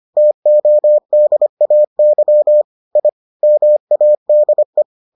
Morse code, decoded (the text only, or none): TODAY I MADE